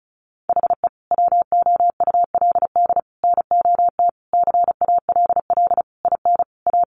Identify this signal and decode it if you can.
Morse code, heard as a keyed tone, about 34 wpm, 700 Hz. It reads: HE WOULD NOT CALL IN A
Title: Sentences from Top 1500 Words 35wpm